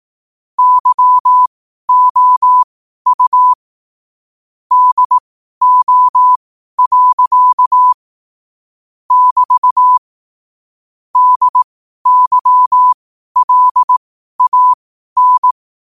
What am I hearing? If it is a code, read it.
Morse code, heard as a keyed tone, about 18 wpm, spaced as for 12 wpm, 1000 Hz. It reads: YOU DO. <BT> DYLAN
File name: qotdfiles / 12wpm